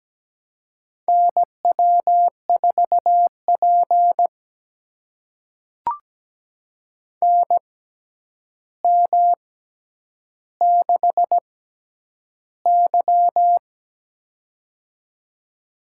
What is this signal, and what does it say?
Morse code, heard as a keyed tone, about 17 wpm, spaced as for 9 wpm, 700 Hz. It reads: NW4P NM6Y
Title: USA Call Signs - 4 Characters - 6x Character Spacing - 17wpm